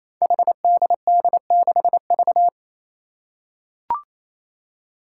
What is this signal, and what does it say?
Morse code, heard as a keyed tone, about 28 wpm, 700 Hz. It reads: HDD6V
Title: Five Challenging Letter-Number Combinations 28wpm